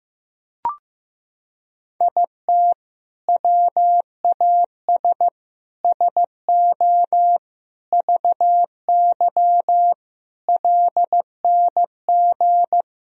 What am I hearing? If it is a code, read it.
Morse code, heard as a keyed tone, about 16 wpm, 700 Hz. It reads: IT WAS SO VY LNG
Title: Sentences from Top 500 Words and Abbreviations 15wpm